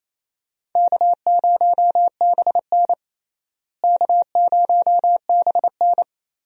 Morse code, heard as a keyed tone, about 28 wpm, 700 Hz. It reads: K0BN K0BN
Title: USA Call Signs - 4 Characters - Repeated 3x - 28wpm